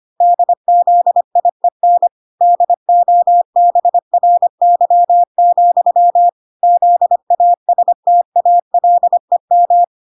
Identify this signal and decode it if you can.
Morse code, heard as a keyed tone, about 25 wpm, 700 Hz. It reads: DZIEN DOBRY, ZASTALEM